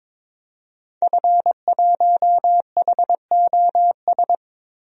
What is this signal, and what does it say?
Morse code, heard as a keyed tone, about 22 wpm, 700 Hz. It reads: F1HOS
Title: DXCC-100 Call Signs - 5 Characters - Repeated 2x - 22wpm